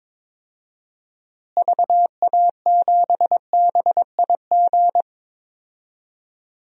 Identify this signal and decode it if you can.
Morse code, heard as a keyed tone, about 22 wpm, 700 Hz. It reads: VA7BIG
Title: DXCC-100 Call Signs - 6 Characters - Repeated 3x - 22wpm